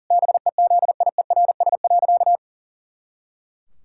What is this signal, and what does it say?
Morse code, heard as a keyed tone, about 40 wpm, 700 Hz. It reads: BEZIERS.
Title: Pangram Challenge v2 - 40wpm